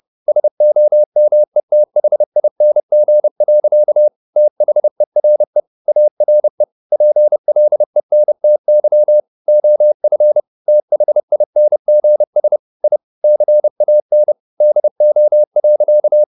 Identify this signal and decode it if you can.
Morse code, heard as a keyed tone, about 30 wpm, 600 Hz. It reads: SOMETHING. THERE ARE PLENTY OF THINGS I CAN DO.
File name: the old man and the sea / the old man and the sea 1 30wpm